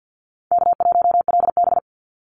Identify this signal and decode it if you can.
Morse code, heard as a keyed tone, about 48 wpm, 700 Hz. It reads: K1LB
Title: Easy Call Signs - 50wpm